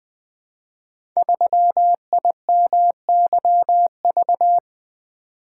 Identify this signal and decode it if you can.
Morse code, heard as a keyed tone, about 20 wpm, 700 Hz. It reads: 3IMYV
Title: Code Groups - 5 Letter:Numbers - 20wpm